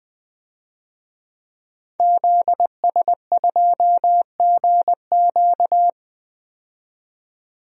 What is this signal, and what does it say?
Morse code, heard as a keyed tone, about 21 wpm, 700 Hz. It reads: ZS2GQ
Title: DXCC-100 Call Signs - 5 Characters - 20wpm